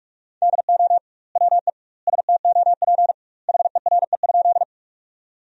Morse code, heard as a keyed tone, about 47 wpm, 700 Hz. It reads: DO WE STOP HERE?
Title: 2x Word-Spacing - Sentences from Top 300 Words - 45wpm